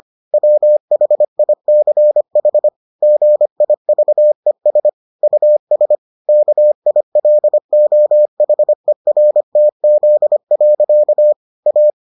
Morse code, heard as a keyed tone, about 25 wpm, 600 Hz. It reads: WHICH GIVES US KILOHERTZ. A